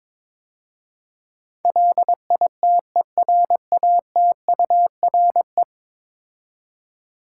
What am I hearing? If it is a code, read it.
Morse code, heard as a keyed tone, about 22 wpm, 700 Hz. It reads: LITERATURE
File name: practice / 22wpm / 10 Letter Words - First 3 Letters Given - 22wpm